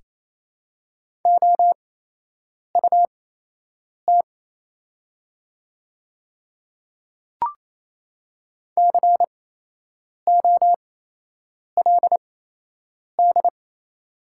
Morse code, heard as a keyed tone, about 28 wpm, spaced as for 8 wpm, 700 Hz. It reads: OUT COLD
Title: 8x Character Spacing - Top 500 Words - 28wpm